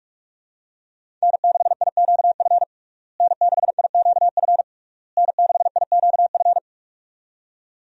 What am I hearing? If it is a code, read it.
Morse code, heard as a keyed tone, about 46 wpm, 700 Hz. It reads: N6IQF N6IQF N6IQF
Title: USA Call Signs - 5 Characters - Repeated 3x - 45wpm